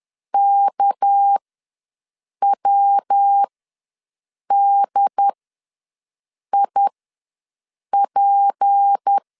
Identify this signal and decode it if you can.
Morse code, heard as a keyed tone, about 10 wpm, spaced as for 6 wpm, 800 Hz. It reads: KWDIP
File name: LEZIONI / 6 PKX0 / 6 PK ws 10wpm 3745